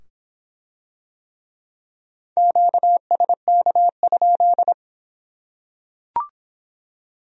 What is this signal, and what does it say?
Morse code, heard as a keyed tone, about 23 wpm, 700 Hz. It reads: QSK?